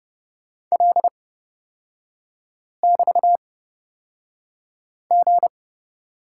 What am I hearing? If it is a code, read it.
Morse code, heard as a keyed tone, about 30 wpm, spaced as for 5 wpm, 700 Hz. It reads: L<BT>G
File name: Lesson 38e - Letter q - Three Characters 30wpm